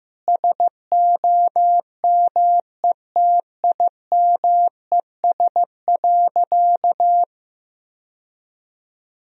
Morse code, heard as a keyed tone, about 15 wpm, 700 Hz. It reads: SOMETIMES.
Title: Sentences from Top 1000 Words - Review Part 3 - 15wpm